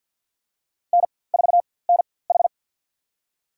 Morse code, heard as a keyed tone, about 47 wpm, spaced as for 23 wpm, 700 Hz. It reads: N4NH
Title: USA Call Signs - 4x Character Spacing - 50wpm